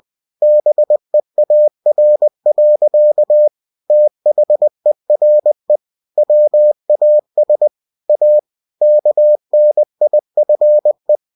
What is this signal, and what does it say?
Morse code, heard as a keyed tone, about 20 wpm, 600 Hz. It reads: BEAR. THERE WAS A KNIFE
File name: cw-books / winnie-the-pooh / winnie-20wpm-page-0109